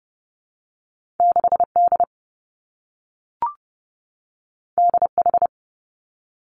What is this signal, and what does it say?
Morse code, heard as a keyed tone, about 30 wpm, 700 Hz. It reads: 6 D D H
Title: Two Challenging Letter-Number Combinations 30wpm